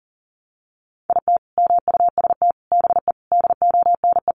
Morse code, heard as a keyed tone, about 37 wpm, 700 Hz. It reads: IT MUST BE DONE
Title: Sentences from Top 1100 Words 40wpm